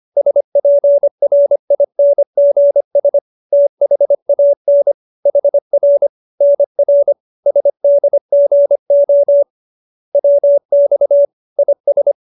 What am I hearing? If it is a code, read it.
Morse code, heard as a keyed tone, about 25 wpm, 570 Hz. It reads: SPRINGS THAN HR NR SDGO WX IS